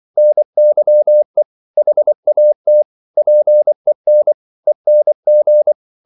Morse code, heard as a keyed tone, about 24 wpm, 600 Hz. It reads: NYE HAT PEN ENG
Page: Korte danske ord 24wpm